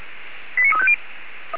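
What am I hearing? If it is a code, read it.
Five-tone selective call (ZVEI), heard as quick runs of five tones, tones of about 70 ms. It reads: 89350
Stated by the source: SDR-received ZVEI five-tone call, code 89347.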